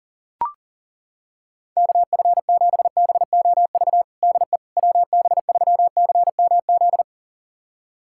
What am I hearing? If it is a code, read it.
Morse code, heard as a keyed tone, about 40 wpm, 700 Hz. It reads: KF7BOV DE WB3CMZ
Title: USA Callsign DE Callsign 40wpm